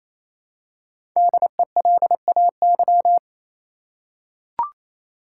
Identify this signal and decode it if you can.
Morse code, heard as a keyed tone, about 28 wpm, 700 Hz. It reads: DELAY
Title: Five Letter Words - 28wpm